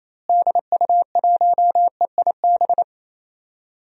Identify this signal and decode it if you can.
Morse code, heard as a keyed tone, about 28 wpm, 700 Hz. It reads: DU1EIB
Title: DXCC-100 Call Signs - 6 Characters - Repeated 3x - 28wpm